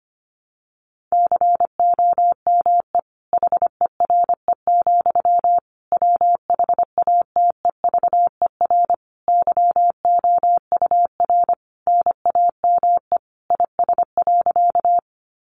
Morse code, heard as a keyed tone, about 25 wpm, 700 Hz. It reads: COME HERE, WHATEVER YOUR NAME IS.
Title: Sentences from Top 1100 Words 25wpm